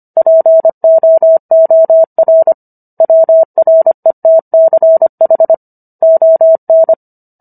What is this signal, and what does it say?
Morse code, heard as a keyed tone, about 25 wpm, 650 Hz. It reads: POOR WRETCH ON